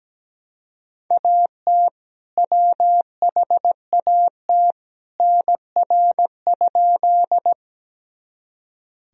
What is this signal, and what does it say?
Morse code, heard as a keyed tone, about 17 wpm, 700 Hz. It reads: AT WHAT NR?
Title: Sentences from Top 100 Words and Abbreviations 17wpm